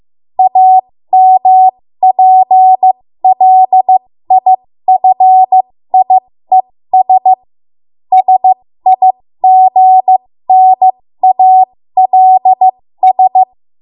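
Morse code, heard as a keyed tone, about 15 wpm, spaced as for 13 wpm, 750 Hz. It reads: AMPLIFIES SIGNALS
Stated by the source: 13 wpm Morse code practice archive transmission.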